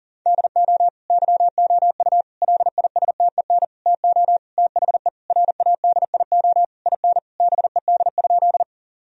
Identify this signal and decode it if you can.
Morse code, heard as a keyed tone, about 40 wpm, 700 Hz. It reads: DO YOU LISTEN TO THE RADIO IN BED?
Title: Sentences from Top 1000 Words - Review Part 3 - 40wpm